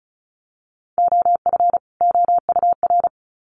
Morse code, heard as a keyed tone, about 34 wpm, 700 Hz. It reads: OF OUR
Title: Sets of 2 Words 35wpm